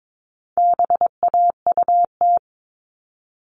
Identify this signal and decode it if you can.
Morse code, heard as a keyed tone, about 22 wpm, 700 Hz. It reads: BAUT
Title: All Cut Numbers - 4-Digit 22wpm